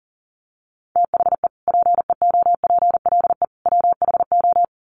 Morse code, heard as a keyed tone, about 40 wpm, 700 Hz. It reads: THE PEOPLE WHO
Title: Sets of 3 Words 40wpm